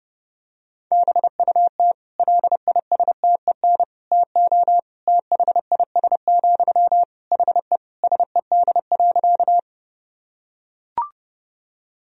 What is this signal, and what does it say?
Morse code, heard as a keyed tone, about 32 wpm, 700 Hz. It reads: BUT LISTEN TO THIS, HE SED.
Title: Sentences from Top 400 Words and Abbreviations 30wpm